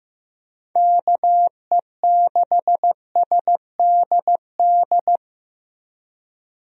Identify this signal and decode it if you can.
Morse code, heard as a keyed tone, about 15 wpm, 700 Hz. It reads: KE6SDD
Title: USA Call Signs - 6 Characters - 15wpm